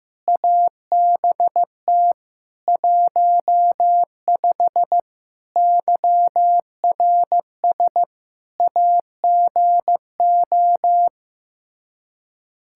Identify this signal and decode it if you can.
Morse code, heard as a keyed tone, about 15 wpm, 700 Hz. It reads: ABT 15 YRS AGO
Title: K1USN SST QSO Practice 15wpm